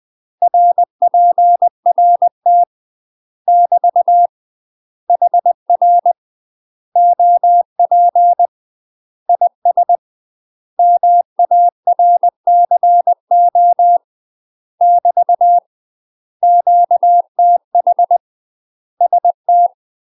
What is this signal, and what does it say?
Morse code, heard as a keyed tone, about 20 wpm, 700 Hz. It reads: RPRT <BT> HR OP IS MARCO <BT> QTH ST
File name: UK MASTER QSOs TRAINING / 20WPM / UK MASTER QSOs 0068